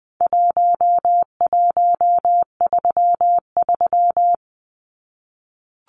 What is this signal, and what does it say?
Morse code, heard as a keyed tone, about 20 wpm, 700 Hz. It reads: 1133
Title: Numbers - Contest Style - 4-Digit 20wpm